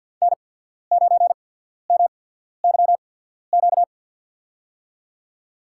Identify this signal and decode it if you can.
Morse code, heard as a keyed tone, about 53 wpm, spaced as for 14 wpm, 700 Hz. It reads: N9MYQ
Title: USA Call Signs - 5 Characters - 8x Character Spacing - 50wpm